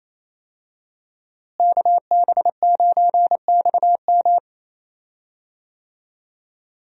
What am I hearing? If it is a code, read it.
Morse code, heard as a keyed tone, about 28 wpm, 700 Hz. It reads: KB9XM
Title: USA Call Signs - 5 Characters - Repeated 3x - 28wpm